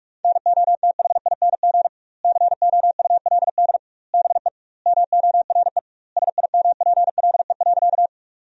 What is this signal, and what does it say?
Morse code, heard as a keyed tone, about 45 wpm, 700 Hz. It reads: NOTHING COULD BE MORE SIMPLE.
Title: 2x Word-Spacing - Sentences from Top 500 Words - 45wpm